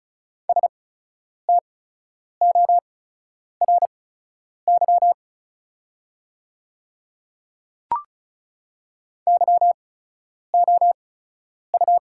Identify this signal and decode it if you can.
Morse code, heard as a keyed tone, about 34 wpm, spaced as for 10 wpm, 700 Hz. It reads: STORY YOU
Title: 8x Character Spacing - Top 300 Words - 35wpm